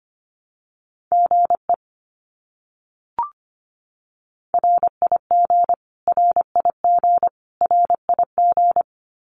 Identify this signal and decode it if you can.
Morse code, heard as a keyed tone, about 25 wpm, 700 Hz. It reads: GE RIG RIG RIG
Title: Abbreviations Repeated 3x - 25wpm